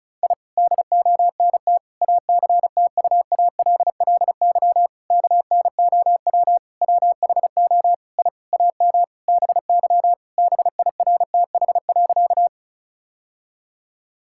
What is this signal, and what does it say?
Morse code, heard as a keyed tone, about 36 wpm, 690 Hz. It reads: I DONT ACTUALLY KNOW WHO I AM BY BIRTH.
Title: Sentences from Top 1000 Words - Review Part 5 - 35wpm